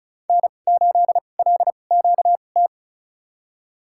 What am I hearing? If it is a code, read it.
Morse code, heard as a keyed tone, about 36 wpm, spaced as for 25 wpm, 700 Hz. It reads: N8LQT
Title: USA Call Signs - 5 Characters - 2x Character Spacing - 35wpm